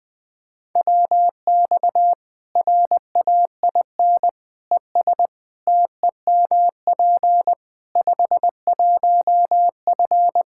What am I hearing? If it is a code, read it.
Morse code, heard as a keyed tone, about 20 wpm, 700 Hz. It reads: WX RAIN ES TEMP 51F
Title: Practice QSO - Exchange 2 - Info - 20wpm